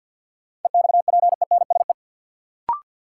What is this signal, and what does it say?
Morse code, heard as a keyed tone, about 50 wpm, 700 Hz. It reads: EXPENSE
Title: Sets of 4 Words - New Word Drill - 50wpm